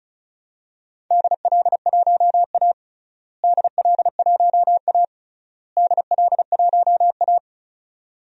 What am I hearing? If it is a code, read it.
Morse code, heard as a keyed tone, about 36 wpm, 700 Hz. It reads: DL1A DL1A DL1A
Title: DXCC-100 Call Signs - 4 Characters - Repeated 3x - 35wpm